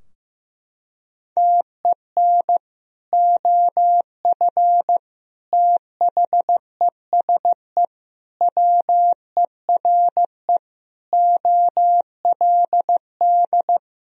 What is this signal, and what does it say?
Morse code, heard as a keyed tone, about 15 wpm, 700 Hz. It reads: TEN OF THESE WERE OLD
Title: Sentences from Top 1000 Words - Review Part 2 - 15wpm